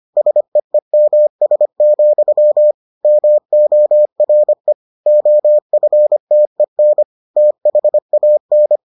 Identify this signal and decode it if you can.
Morse code, heard as a keyed tone, about 25 wpm, 600 Hz. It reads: SEEMS, MORE OFTEN THAN